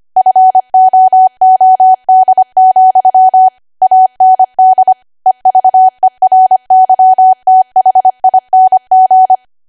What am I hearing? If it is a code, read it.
Morse code, heard as a keyed tone, about 25 wpm, 750 Hz. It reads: FOOD, AND EVERYTHING
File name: Archive / 25 WPM / 25WPM